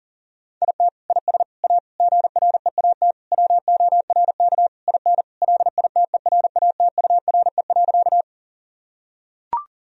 Morse code, heard as a keyed tone, about 40 wpm, 700 Hz. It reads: IT IS A GREAT WORK IN LITERATURE.
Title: Sentences from Top 1000 Words - Review Part 3 - 40wpm